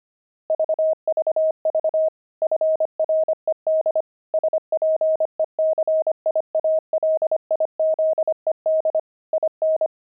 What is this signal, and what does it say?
Morse code, heard as a keyed tone, about 25 wpm, 630 Hz. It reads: VVV FRED SPECIALIZED IN
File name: pangram-fs-25-cs-25-toys